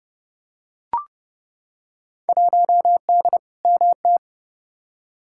Morse code, heard as a keyed tone, about 31 wpm, 700 Hz. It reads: 1D MT
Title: Field Day Exchanges - 30wpm